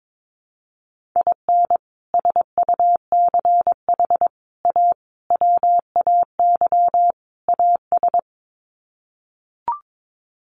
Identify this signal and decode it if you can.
Morse code, heard as a keyed tone, about 22 wpm, 700 Hz. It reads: IN SUCH A WAY AS
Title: Sets of 5 Words Encore 22wpm